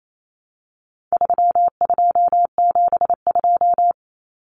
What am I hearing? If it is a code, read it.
Morse code, heard as a keyed tone, about 28 wpm, 700 Hz. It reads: 3272
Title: Common Cut Numbers - 4-Digit 28wpm